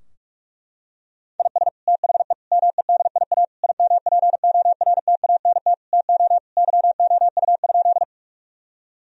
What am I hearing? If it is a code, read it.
Morse code, heard as a keyed tone, about 46 wpm, 700 Hz. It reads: IS THE MEDIA IMPORTANT TO YOU?
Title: Sentences from Top 1000 Words - Review Part 4 - 45wpm